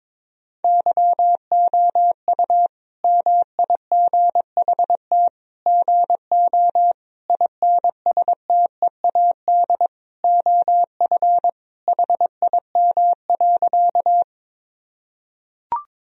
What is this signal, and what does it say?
Morse code, heard as a keyed tone, about 22 wpm, 700 Hz. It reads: YOU MIGHT GO INSTEAD OF HIM.
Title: Sentences from Top 1000 Words - Review Part 4 - 22wpm